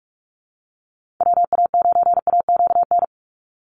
Morse code, heard as a keyed tone, about 44 wpm, 700 Hz. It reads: WA9AQN
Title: Call Signs - 45wpm